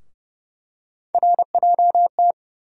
Morse code, heard as a keyed tone, about 31 wpm, 700 Hz. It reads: RJT
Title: USA Call Sign Postfixes - 3 Letters 30wpm